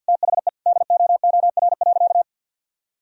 Morse code, heard as a keyed tone, about 51 wpm, 700 Hz. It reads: THE DOOR.
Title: Sentences from Top 1000 Words - Review Part 1 - 50wpm